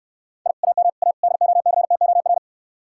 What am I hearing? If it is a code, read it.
Morse code, heard as a keyed tone, about 67 wpm, 690 Hz. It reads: I AM A DOCTOR